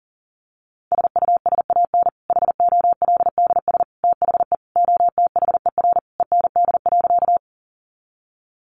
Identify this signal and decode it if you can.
Morse code, heard as a keyed tone, about 40 wpm, 700 Hz. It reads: SUSAN HOLDS THE OTHER END.
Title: Easy Phrases 40wpm